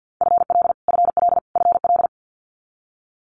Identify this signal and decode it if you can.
Morse code, heard as a keyed tone, about 50 wpm, 700 Hz. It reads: FL FL FL
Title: Multi-letter Phonemes Repeated 3x 50wpm